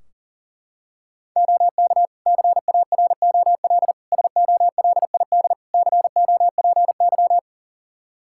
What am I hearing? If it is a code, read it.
Morse code, heard as a keyed tone, about 40 wpm, 700 Hz. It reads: OK CAROL SOLID COPY
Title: K1USN SST QSO Practice 40wpm